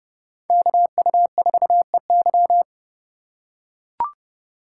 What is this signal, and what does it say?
Morse code, heard as a keyed tone, about 29 wpm, 700 Hz. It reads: KU4EY